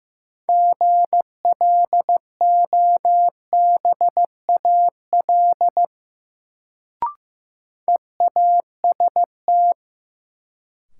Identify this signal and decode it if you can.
Morse code, heard as a keyed tone, about 15 wpm, 700 Hz. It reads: GLOBAL EAST
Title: Sentences from Top 1200 Words - New Word Drill Alt - 15wpm